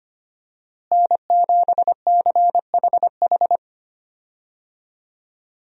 Morse code, heard as a keyed tone, about 25 wpm, 700 Hz. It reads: N7CHH
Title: USA Call Signs - 5 Characters - Repeated 2x - 25wpm